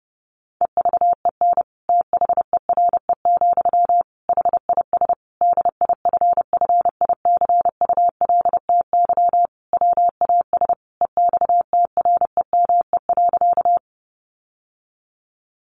Morse code, heard as a keyed tone, about 30 wpm, 700 Hz. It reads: EVEN THERE, HIS DIFFICULTY WAS EXTREME.
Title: Sentences from Top 1500 Words 30wpm